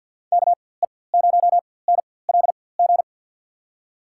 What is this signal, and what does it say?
Morse code, heard as a keyed tone, about 47 wpm, spaced as for 23 wpm, 700 Hz. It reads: KE0NLG
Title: USA Call Signs - 6 Characters - 4x Character Spacing - 50wpm